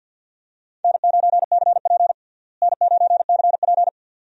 Morse code, heard as a keyed tone, about 53 wpm, 700 Hz. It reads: N9CP N9CP
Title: USA Call Signs - 4 Characters - Repeated 2x - 50wpm